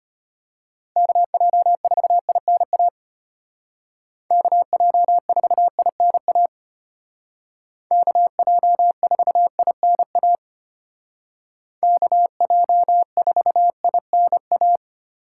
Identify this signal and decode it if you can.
Morse code, heard as a keyed tone, about 30 wpm, 700 Hz. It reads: KJ4INA KJ4INA KJ4INA KJ4INA
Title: Speed Racing - USA 2x3 Call Signs 25wpm